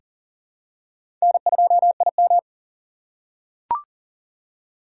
Morse code, heard as a keyed tone, about 40 wpm, 700 Hz. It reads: N2IM
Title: USA Call Signs - 3x Character Spacing - 40wpm